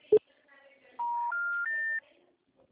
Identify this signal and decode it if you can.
Special information tone, with speech.